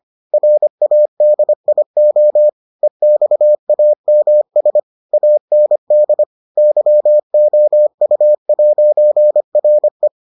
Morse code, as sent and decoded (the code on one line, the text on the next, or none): .-. .- -.. .. --- / . -..- .- -- ... / .- -. -.. / -.-- --- ..- .----. .-. .
RADIO EXAMS AND YOU'RE